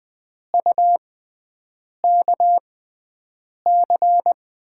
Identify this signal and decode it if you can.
Morse code, heard as a keyed tone, about 20 wpm, spaced as for 7 wpm, 700 Hz. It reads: UKC
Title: USA Call Signs - 5 Characters - 6x Character Spacing - 20wpm